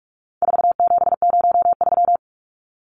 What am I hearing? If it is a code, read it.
Morse code, heard as a keyed tone, about 44 wpm, 700 Hz. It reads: 4703
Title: Numbers - 4-Digit 45wpm